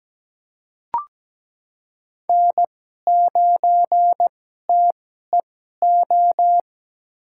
Morse code, heard as a keyed tone, about 17 wpm, spaced as for 12 wpm, 700 Hz. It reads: N9TEO
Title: USA Call Signs - 2x Character Spacing - 17wpm